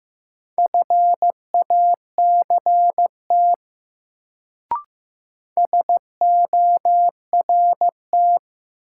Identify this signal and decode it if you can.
Morse code, heard as a keyed tone, about 15 wpm, 700 Hz. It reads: FACT SORT
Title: Sets of 2 Words - New Word Drill - 15wpm